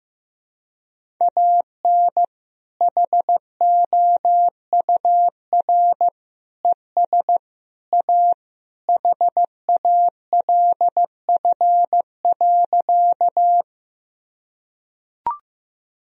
Morse code, heard as a keyed tone, about 15 wpm, 700 Hz. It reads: AN HOUR ES A HALF.
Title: Sentences from Top 400 Words and Abbreviations 15wpm